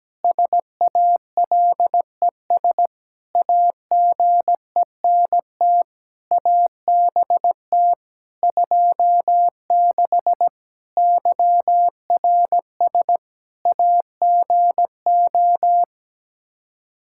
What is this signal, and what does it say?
Morse code, heard as a keyed tone, about 17 wpm, 700 Hz. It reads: SALES AGENT ABT 26 YRS AGO
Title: Practice QSO - Exchange 3 - More Info - 17wpm